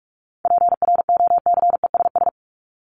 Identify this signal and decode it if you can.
Morse code, heard as a keyed tone, about 44 wpm, 700 Hz. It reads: PROCESS
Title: ES:S Suffix - Top 100 Words with Base Words 45wpm